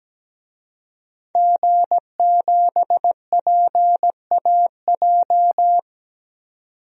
Morse code, heard as a keyed tone, about 17 wpm, 700 Hz. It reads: G7PAJ
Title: DXCC-100 Call Signs - 5 Characters - 17wpm